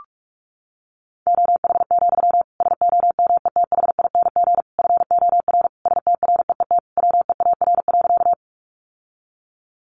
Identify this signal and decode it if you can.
Morse code, heard as a keyed tone, about 44 wpm, 700 Hz. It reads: OH, SOMETHING FOR STREET WEAR.
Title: Sentences from Top 1500 Words 45wpm